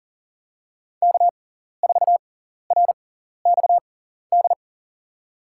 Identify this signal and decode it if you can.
Morse code, heard as a keyed tone, about 40 wpm, spaced as for 14 wpm, 700 Hz. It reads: K4RXD
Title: USA Call Signs - 5 Characters - 6x Character Spacing - 40wpm